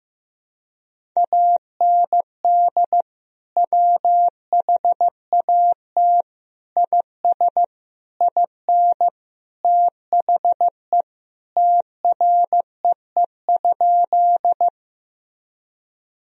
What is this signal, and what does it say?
Morse code, heard as a keyed tone, about 15 wpm, 700 Hz. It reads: AND WHAT IS IN THE TREE?
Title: Sentences from Top 1000 Words - Review Part 1 - 15wpm